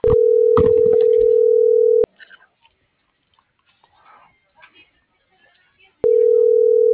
Ringback tone (North America), 440 + 480 Hz.